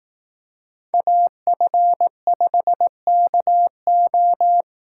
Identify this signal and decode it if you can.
Morse code, heard as a keyed tone, about 18 wpm, 700 Hz. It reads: AF5KO
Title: Speed Racing - USA 2x2 Call Signs 15wpm